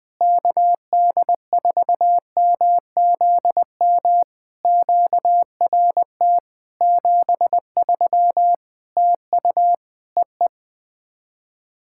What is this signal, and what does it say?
Morse code, heard as a keyed tone, about 20 wpm, 700 Hz. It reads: KD4MZM QRT 73 TU EE
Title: POTA QSO Practice - 20wpm